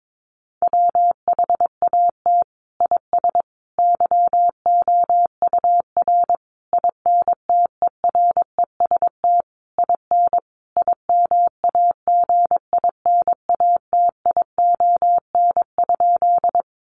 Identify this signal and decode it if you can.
Morse code, heard as a keyed tone, about 22 wpm, 700 Hz. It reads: WHAT IS YOUR INTEREST IN IMAGINATION?
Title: Sentences from Top 1300 Words 22wpm